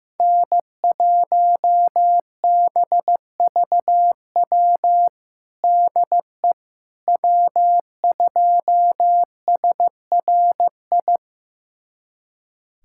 Morse code, heard as a keyed tone, about 15 wpm, 700 Hz. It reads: N1BVW DE W2SRI
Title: USA Callsign DE Callsign - 5 Characters - 15wpm